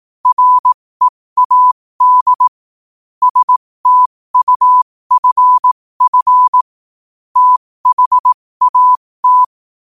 Morse code, heard as a keyed tone, about 18 wpm, spaced as for 15 wpm, 1000 Hz. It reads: READ STUFF THAT
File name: qotdfiles / 15wpm